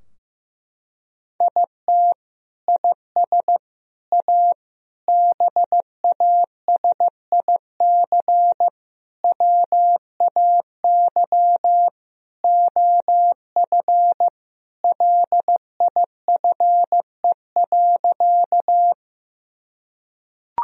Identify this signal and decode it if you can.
Morse code, heard as a keyed tone, about 15 wpm, 700 Hz. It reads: IT IS A BASIC WAY OF LIFE.
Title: Sentences from Top 1000 Words - Review Part 3 - 15wpm